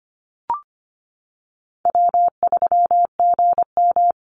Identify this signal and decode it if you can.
Morse code, heard as a keyed tone, about 25 wpm, 700 Hz. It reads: W3GM
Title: USA Call Signs 25wpm